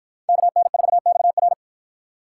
Morse code, heard as a keyed tone, about 50 wpm, 700 Hz. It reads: KN4XR